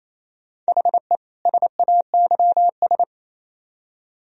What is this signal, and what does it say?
Morse code, heard as a keyed tone, about 36 wpm, 700 Hz. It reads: HE SAYS